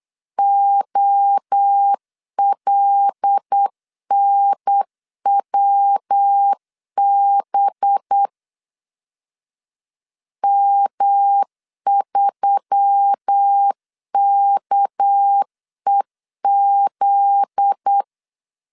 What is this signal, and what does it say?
Morse code, heard as a keyed tone, about 8 wpm, 800 Hz. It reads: OLNWB M3KEZ
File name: LEZIONI / 8 YJ45 / 8 Y 8wpm 3245